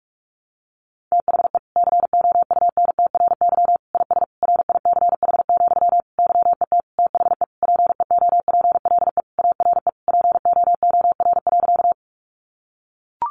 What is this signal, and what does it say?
Morse code, heard as a keyed tone, about 44 wpm, 700 Hz. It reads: THE COUNTRY IS RICH, YET THE PEOPLE ARE POOR.
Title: Sentences from Top 1100 Words 45wpm